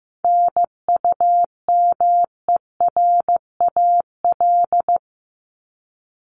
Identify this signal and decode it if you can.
Morse code, heard as a keyed tone, about 15 wpm, 700 Hz. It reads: NUMERAL
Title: Top 500 Words Repeated 15wpm